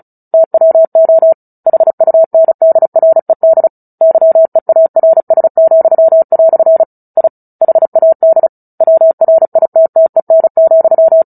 Morse code, heard as a keyed tone, about 34 wpm, 650 Hz. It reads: TWO HUNDRED YEARS," I HAD WRITTEN,
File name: Book Two / 09-Chapter 9 - 35 WPM